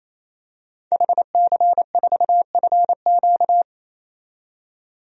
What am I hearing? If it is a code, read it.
Morse code, heard as a keyed tone, about 27 wpm, 700 Hz. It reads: HC4FQ
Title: DXCC-100 Call Signs - 5 Characters - 28wpm